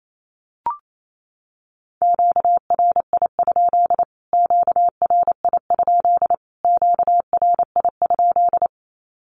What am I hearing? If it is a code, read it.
Morse code, heard as a keyed tone, about 28 wpm, 700 Hz. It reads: QRI? QRI? QRI?
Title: Q-Codes Repeated 3x 28wpm